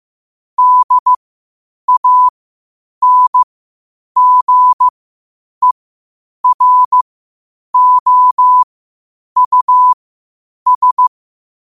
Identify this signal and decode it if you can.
Morse code, heard as a keyed tone, about 15 wpm, spaced as for 8 wpm, 1000 Hz. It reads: DANGEROUS
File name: qotdfiles / 8wpm